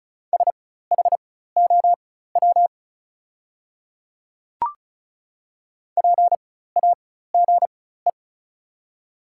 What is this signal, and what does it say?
Morse code, heard as a keyed tone, about 36 wpm, spaced as for 16 wpm, 690 Hz. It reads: SHOW PAGE
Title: 4x Character Spacing - Top 500 Words - 35wpm